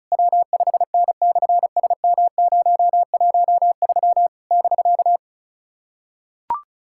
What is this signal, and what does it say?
Morse code, heard as a keyed tone, about 33 wpm, 700 Hz. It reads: W5N/SM013 <BK>
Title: SOTA QSO Practice 35wpm